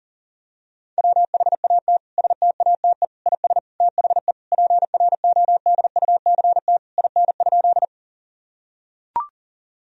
Morse code, heard as a keyed tone, about 37 wpm, 700 Hz. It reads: WHAT STATE IS THE PRODUCT IN?
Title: Sentences from Top 1000 Words - Review Part 2 - 40wpm